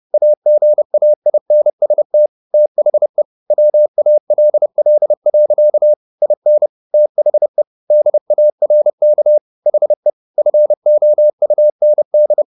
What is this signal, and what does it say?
Morse code, heard as a keyed tone, about 30 wpm, 600 Hz. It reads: AGAINST THE WALL. IN THE DARK HE FOUND